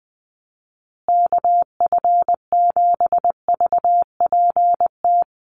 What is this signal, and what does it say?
Morse code, heard as a keyed tone, about 20 wpm, 700 Hz. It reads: KF7VPT
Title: USA Call Signs Repeated 3x 20wpm